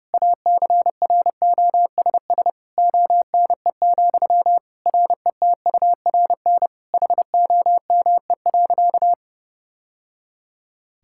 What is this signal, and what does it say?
Morse code, heard as a keyed tone, about 30 wpm, 700 Hz. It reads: ACROSS ONE, RETURN HOME.
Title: Sentences from Top 1000 Words - Review Part 5 - 30wpm